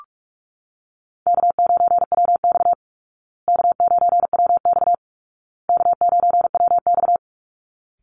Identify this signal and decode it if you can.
Morse code, heard as a keyed tone, about 44 wpm, 700 Hz. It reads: K9WX K9WX K9WX
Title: Easy Call Signs Repeated 3x - 45wpm